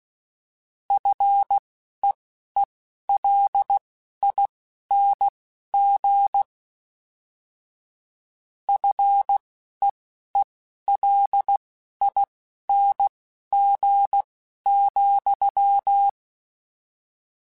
Morse code, heard as a keyed tone, about 16 wpm, spaced as for 11 wpm, 800 Hz. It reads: FEELING FEELING,